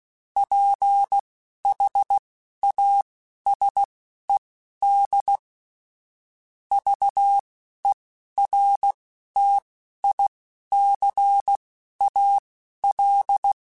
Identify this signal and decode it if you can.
Morse code, heard as a keyed tone, about 16 wpm, spaced as for 12 wpm, 800 Hz. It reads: PHASED VERTICAL